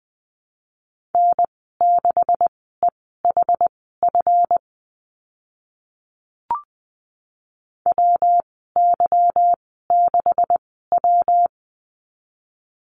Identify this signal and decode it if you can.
Morse code, heard as a keyed tone, about 20 wpm, spaced as for 15 wpm, 700 Hz. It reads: N6EHF WY6W
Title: 147 Lesson 24g - Letter y - Call Signs 20wpm